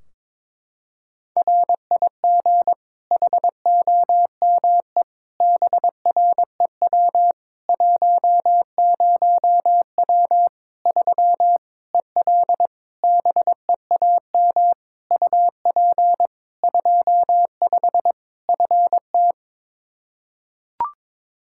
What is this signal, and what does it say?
Morse code, heard as a keyed tone, about 22 wpm, 700 Hz. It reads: RIG HOME BREW 10W 3 EL BEAM UP 25 FT
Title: K1USN SST QSO Practice 22wpm